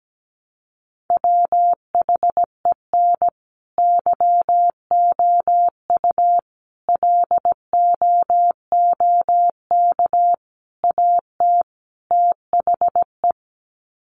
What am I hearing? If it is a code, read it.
Morse code, heard as a keyed tone, about 17 wpm, 700 Hz. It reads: WHEN YOU LOOK AT THE
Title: Sets of 5 Words 17wpm